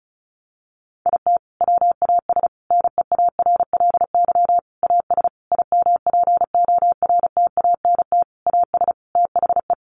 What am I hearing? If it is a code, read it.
Morse code, heard as a keyed tone, about 34 wpm, 700 Hz. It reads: IT WAS NEARLY AS IMPORTANT AS THE
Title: Sentences from Top 1400 Words 35wpm